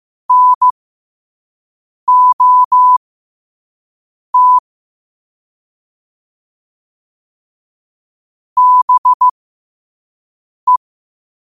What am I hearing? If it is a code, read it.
Morse code, heard as a keyed tone, about 15 wpm, spaced as for 5 wpm, 1000 Hz. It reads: NOT BE
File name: qotdfiles / 5wpm